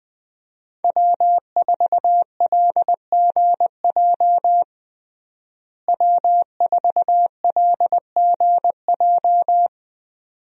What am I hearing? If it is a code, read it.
Morse code, heard as a keyed tone, about 20 wpm, 700 Hz. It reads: W4LGJ W4LGJ
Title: USA Call Signs - 5 Characters - Repeated 2x - 20wpm